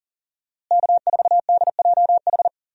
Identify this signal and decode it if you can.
Morse code, heard as a keyed tone, about 40 wpm, 700 Hz. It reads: K4DJH
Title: USA Call Signs - 5 Characters - Repeated 2x - 40wpm